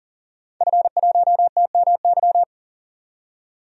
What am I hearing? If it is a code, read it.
Morse code, heard as a keyed tone, about 37 wpm, 700 Hz. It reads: F1TMY
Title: DXCC-100 Call Signs - 5 Characters - 40wpm